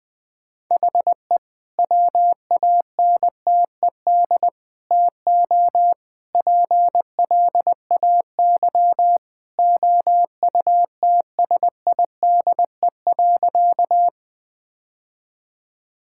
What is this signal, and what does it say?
Morse code, heard as a keyed tone, about 20 wpm, 700 Hz. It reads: HE WANTED TO PLAY OUTSIDE.
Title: Sentences with 3-Word Phrases - Spoken Beginning - 20wpm